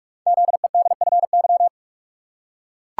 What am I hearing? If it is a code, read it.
Morse code, heard as a keyed tone, about 44 wpm, 690 Hz. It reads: ZEDFY
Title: Code Groups - 5 Letters - 45wpm